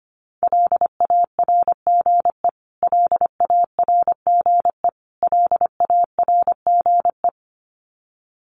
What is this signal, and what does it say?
Morse code, heard as a keyed tone, about 25 wpm, 700 Hz. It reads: LARGE LARGE LARGE
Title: Top 300 Words Repeated 25wpm